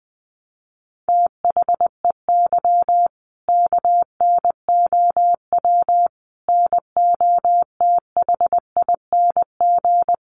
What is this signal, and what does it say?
Morse code, heard as a keyed tone, about 20 wpm, 700 Hz. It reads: THEY KNOW NOTHING
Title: Sentences from Top 500 Words 20wpm